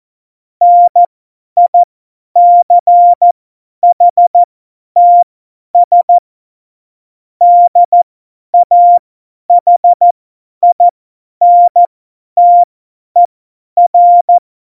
Morse code, heard as a keyed tone, about 14 wpm, spaced as for 10 wpm, 700 Hz. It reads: NICHTS DAHINTER